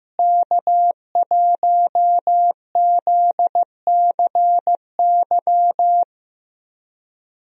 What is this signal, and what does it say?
Morse code, heard as a keyed tone, about 15 wpm, 700 Hz. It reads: K1ZCY
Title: USA Call Signs - 5 Characters - Repeated 3x - 15wpm